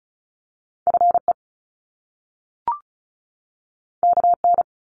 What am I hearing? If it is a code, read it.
Morse code, heard as a keyed tone, about 36 wpm, 700 Hz. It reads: FE KN
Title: Top Two Letter Combinations in Words 35wpm